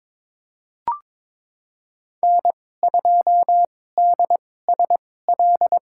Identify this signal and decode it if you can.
Morse code, heard as a keyed tone, about 22 wpm, spaced as for 16 wpm, 700 Hz. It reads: N2DSL
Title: USA Call Signs - 2x Character Spacing - 22wpm